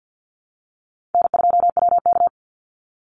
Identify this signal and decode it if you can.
Morse code, heard as a keyed tone, about 49 wpm, 700 Hz. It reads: N2WK
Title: Call Signs Repeated 2x - 50wpm